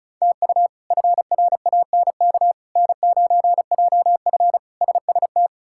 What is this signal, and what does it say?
Morse code, heard as a keyed tone, about 34 wpm, 700 Hz. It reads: TU FRANK N9JF SST
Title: K1USN SST QSO Practice - 35wpm